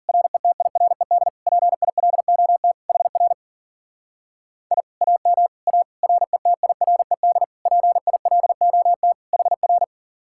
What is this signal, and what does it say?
Morse code, heard as a keyed tone, about 52 wpm, 700 Hz. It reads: RETIRED PILOT HR I AM A RETIRED PILOT HR